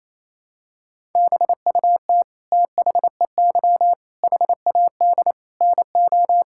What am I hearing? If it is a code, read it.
Morse code, heard as a keyed tone, about 27 wpm, 700 Hz. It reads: BUT THEY HAD NO
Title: Sentences from Top 1000 Words - Review Part 6 - 28wpm